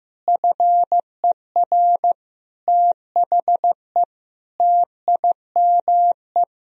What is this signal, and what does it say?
Morse code, heard as a keyed tone, about 14 wpm, 700 Hz. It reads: FER THE TIME
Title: Sentences from Top 500 Words and Abbreviations 15wpm